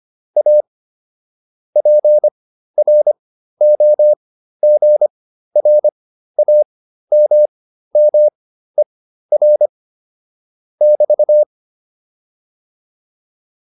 Morse code, heard as a keyed tone, about 25 wpm, spaced as for 13 wpm, 600 Hz. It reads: A PROGRAMMER <BT>